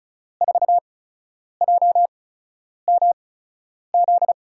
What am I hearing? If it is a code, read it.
Morse code, heard as a keyed tone, about 36 wpm, spaced as for 10 wpm, 700 Hz. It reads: 4JMZ
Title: USA Call Signs - 6 Characters - 8x Character Spacing - 35wpm